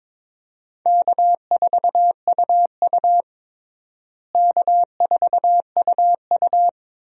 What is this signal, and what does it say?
Morse code, heard as a keyed tone, about 22 wpm, 700 Hz. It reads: K4UU K4UU
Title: USA Call Signs - 4 Characters - Repeated 2x - 22wpm